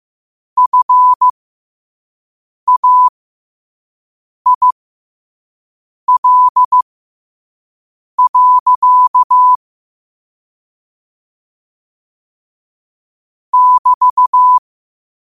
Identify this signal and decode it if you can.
Morse code, heard as a keyed tone, about 15 wpm, spaced as for 5 wpm, 1000 Hz. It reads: FAIL. <BT>
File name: qotdfiles / 5wpm